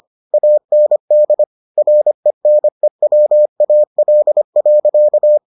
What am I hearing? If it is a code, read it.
Morse code, heard as a keyed tone, about 25 wpm, 600 Hz. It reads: AND RENEWAL.